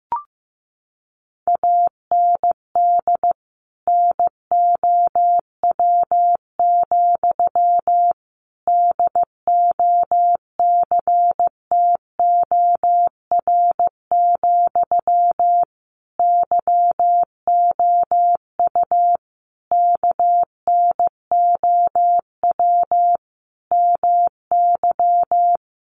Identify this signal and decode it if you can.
Morse code, heard as a keyed tone, about 15 wpm, 700 Hz. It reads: AND NOW, DOCTOR, YOU KNOW MY
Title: Sentences from Top 1500 Words 15wpm